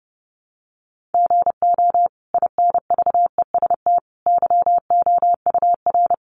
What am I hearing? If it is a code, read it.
Morse code, heard as a keyed tone, about 30 wpm, 700 Hz. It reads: GO INVEST YOUR
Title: Sentences from Top 1100 Words 30wpm